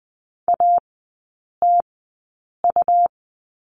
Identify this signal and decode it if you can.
Morse code, heard as a keyed tone, about 20 wpm, spaced as for 8 wpm, 700 Hz. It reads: ATU
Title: 107 Lesson 18e - Letter m - Three Characters 20wpm